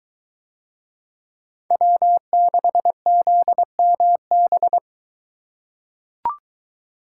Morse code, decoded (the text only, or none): W6ZMB